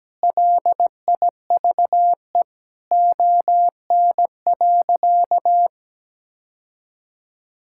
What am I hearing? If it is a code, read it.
Morse code, heard as a keyed tone, about 17 wpm, 700 Hz. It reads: LIVE ON.
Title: Sentences from Top 1000 Words - Review Part 5 - 17wpm